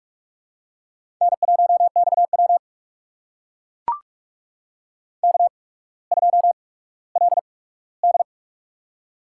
Morse code, heard as a keyed tone, about 44 wpm, spaced as for 19 wpm, 700 Hz. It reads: N1XW K2LD
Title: USA Call Signs - 4 Characters - 8x Character Spacing - 45wpm